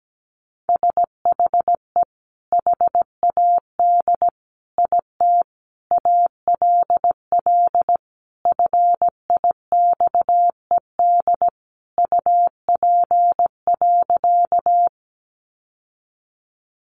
Morse code, heard as a keyed tone, about 17 wpm, 700 Hz. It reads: SHE HAD IT ALL FIXED UP.
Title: Sentences from Top 1100 Words 17wpm